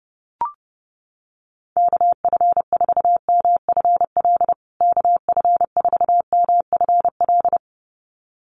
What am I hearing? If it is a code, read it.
Morse code, heard as a keyed tone, about 30 wpm, 700 Hz. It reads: KF4MFL KF4MFL
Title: USA Call Signs Repeated 2x 30wpm